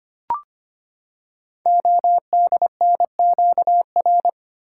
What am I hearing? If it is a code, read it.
Morse code, heard as a keyed tone, about 25 wpm, 700 Hz. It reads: ODNQR
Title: Code Groups - 5 Letters - 25wpm